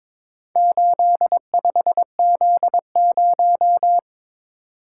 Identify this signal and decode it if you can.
Morse code, heard as a keyed tone, about 22 wpm, 700 Hz. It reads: 85Z0
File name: practice / 22wpm / Four Letter-Number Combinations - 22wpm